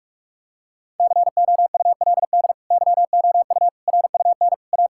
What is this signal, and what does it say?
Morse code, heard as a keyed tone, about 46 wpm, 700 Hz. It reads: COULD YOU RUN A
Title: Sentences from Top 1600 Words 45wpm